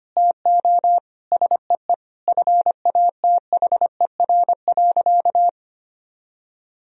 Morse code, heard as a keyed tone, about 25 wpm, 700 Hz. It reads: TO SEE FATHER.
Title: Sentences from Top 1000 Words - Review Part 1 - 25wpm